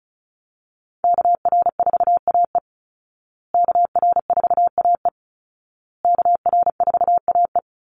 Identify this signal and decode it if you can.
Morse code, heard as a keyed tone, about 34 wpm, 700 Hz. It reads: KR4AE KR4AE KR4AE
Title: Call Signs Repeated 2x - 35wpm